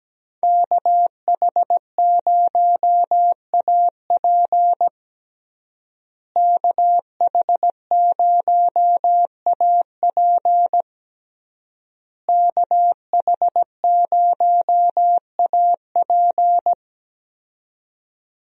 Morse code, heard as a keyed tone, about 17 wpm, 700 Hz. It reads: KH0AP KH0AP KH0AP
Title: USA Call Signs - 5 Characters - Repeated 3x - 17wpm